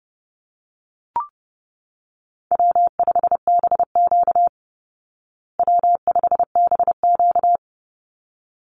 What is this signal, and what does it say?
Morse code, heard as a keyed tone, about 30 wpm, 700 Hz. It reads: W5BQ W5BQ
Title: Call Signs Repeated 3x - 30wpm